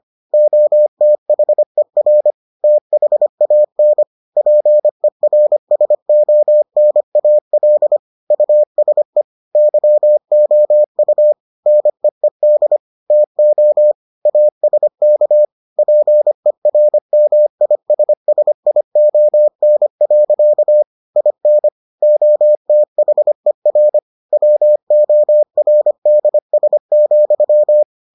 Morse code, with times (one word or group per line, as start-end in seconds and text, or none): 0.3-2.3 OTHER
2.6-4.0 THAN
4.4-8.0 PERSONAL
8.3-9.2 USE
9.6-11.3 YOU
11.7-12.8 NEED
13.1-13.9 TO
14.3-15.4 ASK
15.8-20.8 PERMISSION.
21.2-21.7 IN
22.0-24.0 OTHER
24.3-27.8 WORDS,